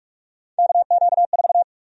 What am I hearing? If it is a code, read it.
Morse code, heard as a keyed tone, about 46 wpm, 700 Hz. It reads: KQ4
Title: USA Call Sign Prefixes - 45wpm